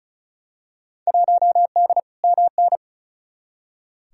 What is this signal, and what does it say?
Morse code, heard as a keyed tone, about 36 wpm, 700 Hz. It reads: 1D MN
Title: Field Day Exchanges - 35wpm